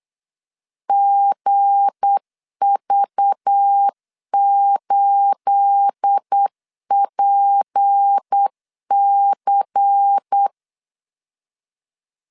Morse code, heard as a keyed tone, about 8 wpm, 800 Hz. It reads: GV8PC
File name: 8wpm